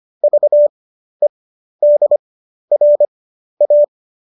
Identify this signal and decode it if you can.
Morse code, heard as a keyed tone, about 25 wpm, spaced as for 12 wpm, 590 Hz. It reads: VEDRA
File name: qso009 12